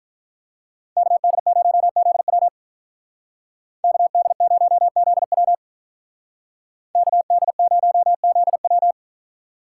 Morse code, heard as a keyed tone, about 42 wpm, 700 Hz. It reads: KD0ZW KD0ZW KD0ZW
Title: Speed Racing - USA Call Signs 35wpm